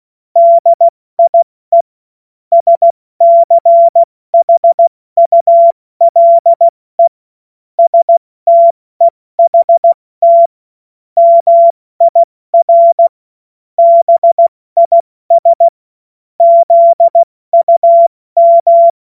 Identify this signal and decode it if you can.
Morse code, heard as a keyed tone, about 16 wpm, 680 Hz. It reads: DIE SCHULE STEHT MIR BIS ZUM
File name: Morsekurs / Witze / Witz 13. September 2022 70 F80